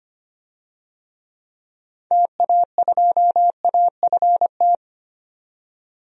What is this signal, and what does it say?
Morse code, heard as a keyed tone, about 25 wpm, 700 Hz. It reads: TA2AFT
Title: DXCC-100 Call Signs - 6 Characters - 25wpm